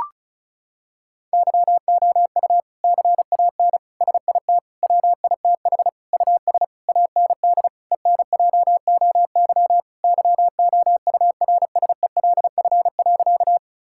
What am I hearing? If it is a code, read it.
Morse code, heard as a keyed tone, about 34 wpm, 700 Hz. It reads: YOU CAN SIT WITH US AND ENJOY YOURSELF.
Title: Sentences from Top 1000 Words - Review Part 5 - 35wpm